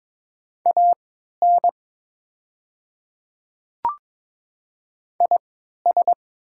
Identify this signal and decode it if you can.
Morse code, heard as a keyed tone, about 22 wpm, spaced as for 12 wpm, 700 Hz. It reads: AN IS
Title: 3x Character Spacing - Top 100 Words - 22wpm